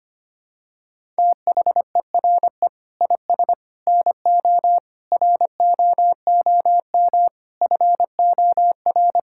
Morse code, heard as a keyed tone, about 25 wpm, 700 Hz. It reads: THERE IS NO ROOM FOR
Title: Sentences from Top 1000 Words - Review Part 5 - 25wpm